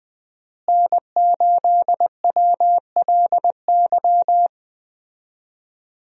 Morse code, heard as a keyed tone, about 20 wpm, 700 Hz. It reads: N8WLY
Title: USA Call Signs - 5 Characters - Repeated 3x - 20wpm